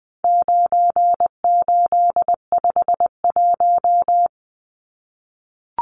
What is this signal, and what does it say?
Morse code, heard as a keyed tone, about 20 wpm, 700 Hz. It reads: 9851
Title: Numbers - 4-Digit 20wpm